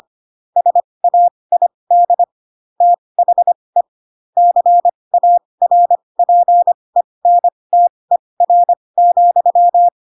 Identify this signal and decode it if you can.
Morse code, heard as a keyed tone, about 25 wpm, spaced as for 20 wpm, 700 Hz. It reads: SAID THE CARPENTER,